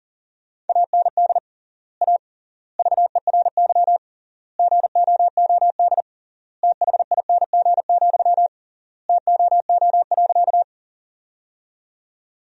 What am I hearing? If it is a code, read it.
Morse code, heard as a keyed tone, about 40 wpm, 700 Hz. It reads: AND A VERY GOOD THING, TOO.
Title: 3x Word-Spacing - Sentences from Top 200 Words - 40wpm